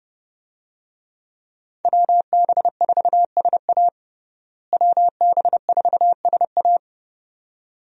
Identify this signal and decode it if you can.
Morse code, heard as a keyed tone, about 30 wpm, 700 Hz. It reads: WB4SA WB4SA
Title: USA Call Signs - 5 Characters - Repeated 2x - 30wpm